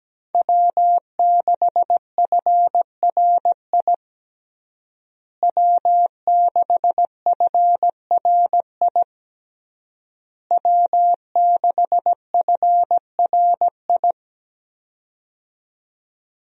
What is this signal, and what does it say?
Morse code, heard as a keyed tone, about 17 wpm, 700 Hz. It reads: W6FRI W6FRI W6FRI
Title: USA Call Signs - 5 Characters - Repeated 3x - 17wpm